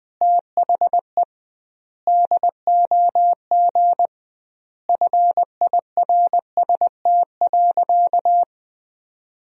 Morse code, heard as a keyed tone, about 20 wpm, 700 Hz. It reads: THE DOG FIRST.
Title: 2x Word-Spacing - Sentences from Top 400 Words - 20wpm